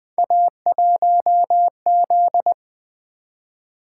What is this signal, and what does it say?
Morse code, heard as a keyed tone, about 20 wpm, 700 Hz. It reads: A1Z
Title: Three Letter-Number Combinations - 20wpm